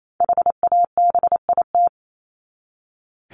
Morse code, heard as a keyed tone, about 28 wpm, 700 Hz. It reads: HABIT
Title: Top 1751-2000 Words 28wpm